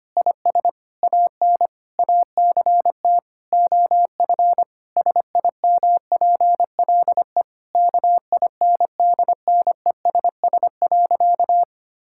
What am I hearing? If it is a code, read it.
Morse code, heard as a keyed tone, about 25 wpm, 700 Hz. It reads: IS AN ACT OF SIMPLE KINDNESS.
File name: practice / 25wpm / Difficult Sentences with Hints - 25wpm